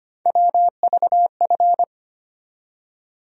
Morse code, heard as a keyed tone, about 25 wpm, 700 Hz. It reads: WVF
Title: USA Call Sign Postfixes - 3 Letters 25wpm